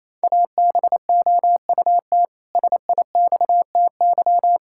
Morse code, heard as a keyed tone, about 28 wpm, 700 Hz. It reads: ABOUT SIXTY
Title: Pangram Challenge v2 - 28wpm